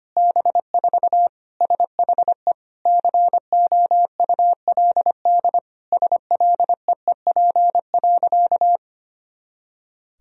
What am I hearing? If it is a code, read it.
Morse code, heard as a keyed tone, about 25 wpm, 700 Hz. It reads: B4 SHE COULD SLEEP.
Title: Sentences from Top 500 Words and Abbreviations 25wpm